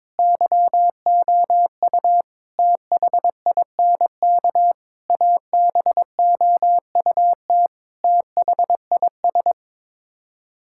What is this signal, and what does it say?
Morse code, heard as a keyed tone, about 22 wpm, 700 Hz. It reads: YOU THINK ABOUT THIS
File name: Sets of 9 Words - 22wpm